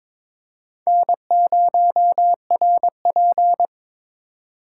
Morse code, heard as a keyed tone, about 22 wpm, 700 Hz. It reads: N0RP
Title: USA Call Signs - 2x Character Spacing - 22wpm